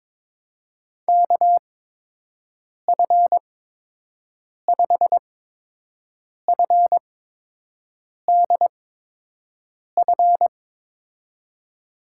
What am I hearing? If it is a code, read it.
Morse code, heard as a keyed tone, about 22 wpm, spaced as for 6 wpm, 700 Hz. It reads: KF5FDF
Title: USA Call Signs - 8x Character Spacing - 22wpm